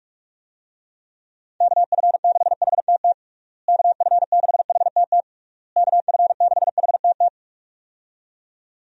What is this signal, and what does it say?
Morse code, heard as a keyed tone, about 41 wpm, 700 Hz. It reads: KF6HTT KF6HTT KF6HTT
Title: USA Call Signs - 6 Characters - Repeated 3x - 45wpm